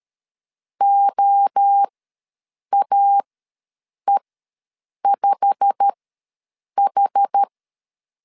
Morse code, heard as a keyed tone, about 13 wpm, spaced as for 7 wpm, 800 Hz. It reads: OAE5H
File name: LEZIONI / 8 YJ45 / 8 YJ45 ws 12wpm 3146